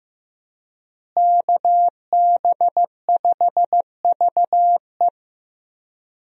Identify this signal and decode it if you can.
Morse code, heard as a keyed tone, about 15 wpm, 700 Hz. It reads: KB5VE
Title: USA Call Signs - 5 Characters - Repeated 3x - 15wpm